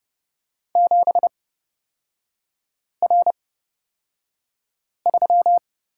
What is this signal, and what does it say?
Morse code, heard as a keyed tone, about 30 wpm, spaced as for 5 wpm, 700 Hz. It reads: 7R3
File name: Lesson 36e - BT - Three Characters 30wpm